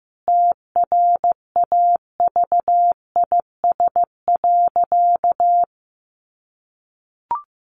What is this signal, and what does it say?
Morse code, heard as a keyed tone, about 15 wpm, 700 Hz. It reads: TRAVIS.
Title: A Sound Of Thunder 15wpm